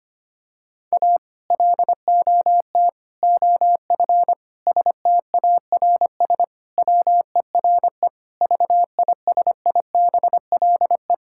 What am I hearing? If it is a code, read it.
Morse code, heard as a keyed tone, about 25 wpm, 700 Hz. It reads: A LOT OF STARS WERE VISIBLE
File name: practice / 25wpm / Sentences with 2-Word Phrases - 25wpm